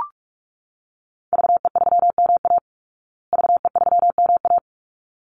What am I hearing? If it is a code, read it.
Morse code, heard as a keyed tone, about 44 wpm, 700 Hz. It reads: VE3MA VE3MA
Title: Call Signs Repeated 3x - 45wpm